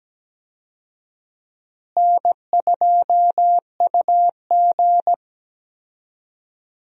Morse code, heard as a keyed tone, about 17 wpm, 700 Hz. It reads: N2UG
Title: USA Call Signs - 4 Characters - Repeated 2x - 17wpm